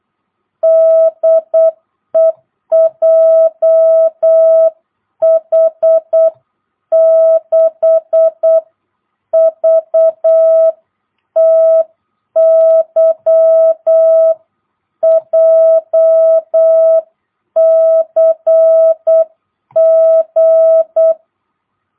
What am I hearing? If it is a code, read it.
Morse code, heard as a keyed tone, about 8 wpm, 660 Hz. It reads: DE JH6VTY JCG
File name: jh8cbh / 2022cw